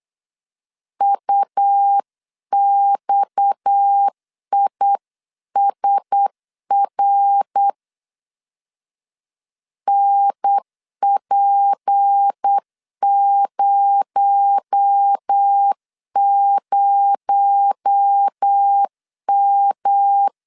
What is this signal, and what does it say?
Morse code, heard as a keyed tone, about 8 wpm, 800 Hz. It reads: UXISR NP00M
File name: CW / DET / 1 6 1500C S B 8wpm 5142